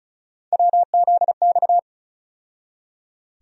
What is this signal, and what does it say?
Morse code, heard as a keyed tone, about 34 wpm, 700 Hz. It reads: WZX
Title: USA Call Sign Postfixes - 3 Letters 35wpm